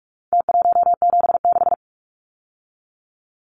Morse code, heard as a keyed tone, about 44 wpm, 700 Hz. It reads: T176
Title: Numbers - Contest Style - 4-Digit 45wpm